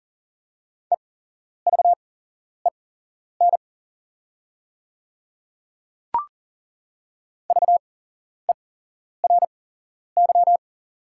Morse code, heard as a keyed tone, about 41 wpm, spaced as for 11 wpm, 700 Hz. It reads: EVEN VERY
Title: 8x Character Spacing - Top 200 Words - 40wpm